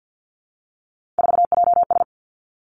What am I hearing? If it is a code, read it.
Morse code, heard as a keyed tone, about 50 wpm, 700 Hz. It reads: 4JS
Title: Three Challenging Letter-Number Combinations 50wpm